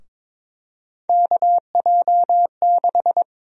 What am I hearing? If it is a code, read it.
Morse code, heard as a keyed tone, about 22 wpm, 700 Hz. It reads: KJ6
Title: USA Call Sign Prefixes - 22wpm